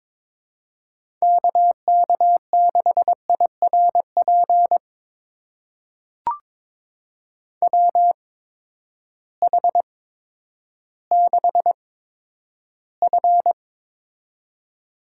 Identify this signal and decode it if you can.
Morse code, heard as a keyed tone, about 22 wpm, spaced as for 11 wpm, 700 Hz. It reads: KK6IRP WH6F
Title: USA Call Signs - 8x Character Spacing - 22wpm